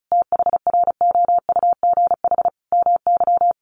Morse code, heard as a keyed tone, about 34 wpm, 700 Hz. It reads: THROUGH MY
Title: Sentences from Top 1300 Words 35wpm